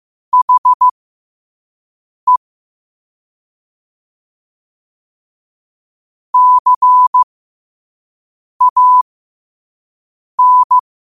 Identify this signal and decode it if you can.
Morse code, heard as a keyed tone, about 15 wpm, spaced as for 5 wpm, 1000 Hz. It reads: HE CAN
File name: qotdfiles / 5wpm